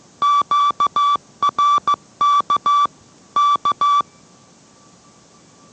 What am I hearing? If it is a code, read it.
Morse code, heard as a keyed tone, about 15 wpm, 1160 Hz. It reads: QRK K